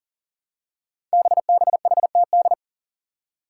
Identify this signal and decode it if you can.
Morse code, heard as a keyed tone, about 37 wpm, 700 Hz. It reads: BBHTD
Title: Five Challenging Letter-Number Combinations 40wpm